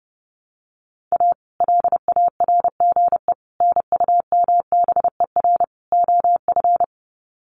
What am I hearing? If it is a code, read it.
Morse code, heard as a keyed tone, about 30 wpm, 700 Hz. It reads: A LARGE NUMBER OF
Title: Sets of 4 Words 30wpm